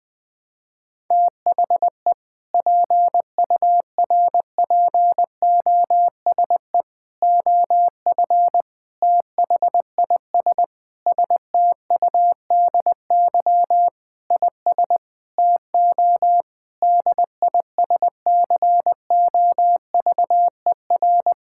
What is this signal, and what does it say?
Morse code, heard as a keyed tone, about 20 wpm, 700 Hz. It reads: THE PURPOSE OF THIS STUDY IS TO DISCOVER
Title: Sentences with 5-Word Phrases - 20wpm